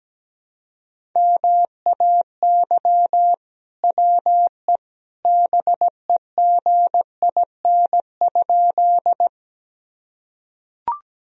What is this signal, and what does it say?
Morse code, heard as a keyed tone, about 17 wpm, 700 Hz. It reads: MAY WE BEGIN?
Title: Sentences from Top 1000 Words - Review Part 1 - 17wpm